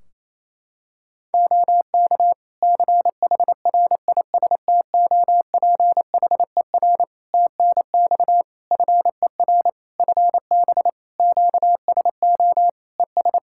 Morse code, heard as a keyed tone, about 27 wpm, 700 Hz. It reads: OK CHRISTOPHER TNX FER FB QSO ES
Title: Practice QSO - Exchange 4 - Ending - 28wpm